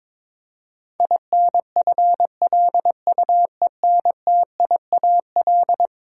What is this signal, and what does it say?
Morse code, heard as a keyed tone, about 22 wpm, 700 Hz. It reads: INFLUENTIAL
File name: practice / 22wpm / Eleven Letter Words - 22wpm